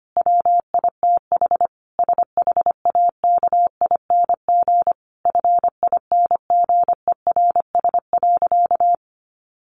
Morse code, heard as a keyed tone, about 25 wpm, 700 Hz. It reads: WITH SHAKING FINGERS.
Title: A Sound Of Thunder 25wpm